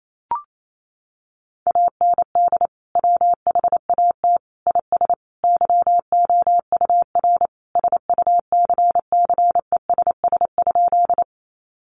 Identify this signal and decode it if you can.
Morse code, heard as a keyed tone, about 28 wpm, 700 Hz. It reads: AND WHAT IS YOUR SUCCESS?
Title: Sentences from Top 900 Words 28wpm